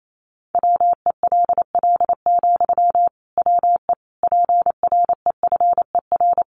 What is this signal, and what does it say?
Morse code, heard as a keyed tone, about 28 wpm, 700 Hz. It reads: WELL, WE PREFER
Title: Sentences from Top 1100 Words 28wpm